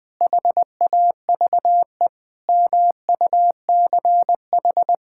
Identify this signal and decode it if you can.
Morse code, heard as a keyed tone, about 21 wpm, 700 Hz. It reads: HAVE MUCH
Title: Sentences from Top 1000 Words - Review Part 6 - 20wpm